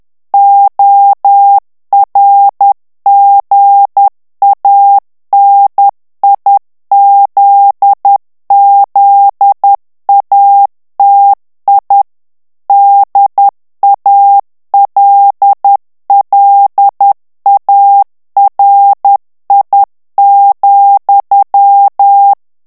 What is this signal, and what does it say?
Morse code, heard as a keyed tone, about 10 wpm, 800 Hz. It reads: ORGANIZZATI DALLARI,
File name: ITALIEN texte clair 2 vitesse 10 mots minute